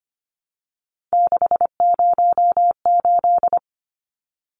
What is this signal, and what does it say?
Morse code, heard as a keyed tone, about 25 wpm, 690 Hz. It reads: 608
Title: Numbers - 3-Digit 25wpm